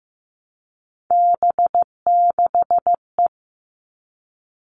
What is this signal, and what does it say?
Morse code, heard as a keyed tone, about 15 wpm, 700 Hz. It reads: B6E
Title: All Cut Numbers - 3-Digit 15wpm